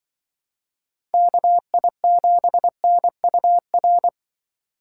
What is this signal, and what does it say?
Morse code, heard as a keyed tone, about 25 wpm, 700 Hz. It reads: KI7NUR
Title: Speed Racing - USA 2x3 Call Signs 20wpm